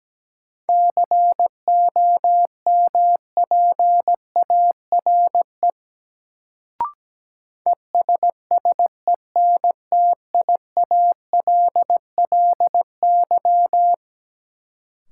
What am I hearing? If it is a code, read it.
Morse code, heard as a keyed tone, about 17 wpm, 700 Hz. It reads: COMPARE ESSENTIALLY
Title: Sentences from Top 1300 Words - New Word Drill Alt - 17wpm